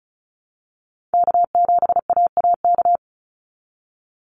Morse code, heard as a keyed tone, about 34 wpm, 700 Hz. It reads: K7AAK
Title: USA Call Signs 35wpm